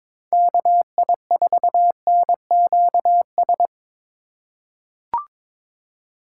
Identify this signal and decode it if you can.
Morse code, heard as a keyed tone, about 21 wpm, 700 Hz. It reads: KI4NQS
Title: USA Call Signs - 2x Character Spacing - 22wpm